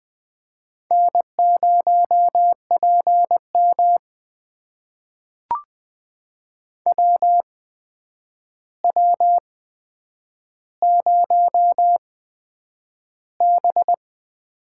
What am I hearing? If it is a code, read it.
Morse code, heard as a keyed tone, about 20 wpm, spaced as for 9 wpm, 700 Hz. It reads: N0PM WW0B
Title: USA Call Signs - 4 Characters - 8x Character Spacing - 20wpm